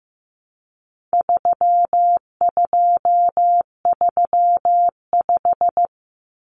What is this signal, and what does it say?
Morse code, heard as a keyed tone, about 15 wpm, 700 Hz. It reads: 3235
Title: Numbers - 4-Digit 15wpm